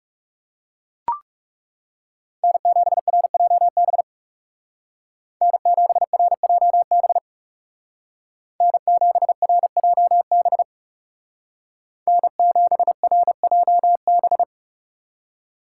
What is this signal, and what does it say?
Morse code, heard as a keyed tone, about 37 wpm, 700 Hz. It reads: N7RJB N7RJB N7RJB N7RJB
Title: Speed Racing - USA 1x3 Call Signs 30wpm